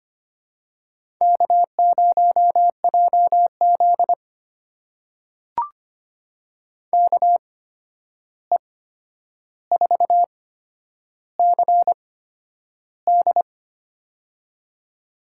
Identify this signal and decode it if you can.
Morse code, heard as a keyed tone, about 25 wpm, spaced as for 10 wpm, 700 Hz. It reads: K0JZ KE4CD
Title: USA Call Signs - 8x Character Spacing - 25wpm